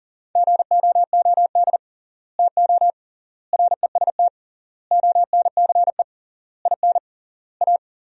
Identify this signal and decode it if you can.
Morse code, heard as a keyed tone, about 37 wpm, 700 Hz. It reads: GOOD TO REST ONCE IN A
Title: 3x Word-Spacing - Sentences from Top 500 Words - 40wpm